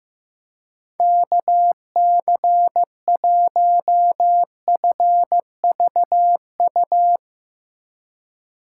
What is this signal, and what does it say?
Morse code, heard as a keyed tone, about 15 wpm, 700 Hz. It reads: KC1FVU
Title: Speed Racing - USA 2x3 Call Signs 15wpm